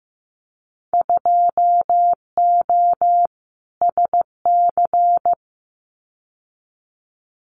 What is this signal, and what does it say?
Morse code, heard as a keyed tone, about 15 wpm, 700 Hz. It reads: 2O SC
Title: Winter Field Day Exchanges 15wpm